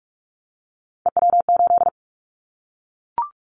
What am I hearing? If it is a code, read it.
Morse code, heard as a keyed tone, about 44 wpm, 700 Hz. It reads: EW8
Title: Callsign Prefixes for Top 100 DXCC - 45wpm